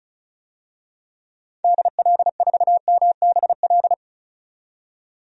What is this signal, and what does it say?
Morse code, heard as a keyed tone, about 36 wpm, 700 Hz. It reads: DL4MBL
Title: DXCC-100 Call Signs - 6 Characters - 35wpm